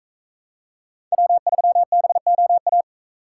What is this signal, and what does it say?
Morse code, heard as a keyed tone, about 39 wpm, 700 Hz. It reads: W3BOA